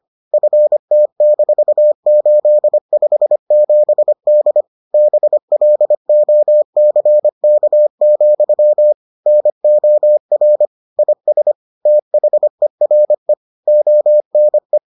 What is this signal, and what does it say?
Morse code, heard as a keyed tone, about 25 wpm, 600 Hz. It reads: FT-857D BLOCK, NOR IS THERE ONE